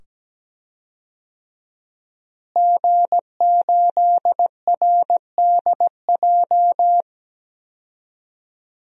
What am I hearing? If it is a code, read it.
Morse code, heard as a keyed tone, about 17 wpm, 700 Hz. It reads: G8RDJ
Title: DXCC-100 Call Signs - 5 Characters - Repeated 2x - 17wpm